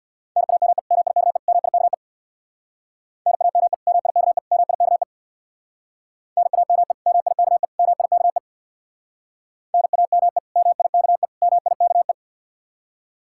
Morse code, heard as a keyed tone, about 66 wpm, 700 Hz. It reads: NAME MIKE MIKE NAME MIKE MIKE NAME MIKE MIKE NAME MIKE MIKE